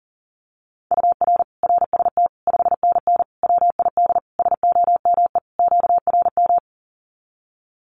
Morse code, heard as a keyed tone, about 40 wpm, 700 Hz. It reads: UR RST 5NN WID SOME QRM
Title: Common QSO Phrases 40wpm